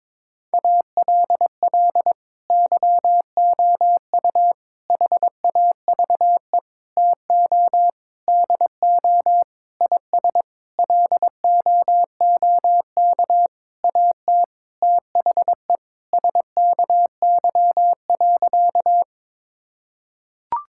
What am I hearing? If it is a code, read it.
Morse code, heard as a keyed tone, about 22 wpm, 700 Hz. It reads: ALL YOU HAVE TO DO IS LOOK AT THE SKY.
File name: Sentences with 9-Word Phrases - 22wpm